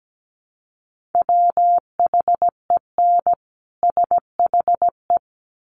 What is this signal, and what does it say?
Morse code, heard as a keyed tone, about 17 wpm, 700 Hz. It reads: WHEN SHE
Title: Sets of 2 Words 17wpm